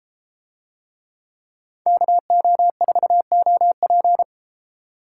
Morse code, heard as a keyed tone, about 33 wpm, 700 Hz. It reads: KO4OP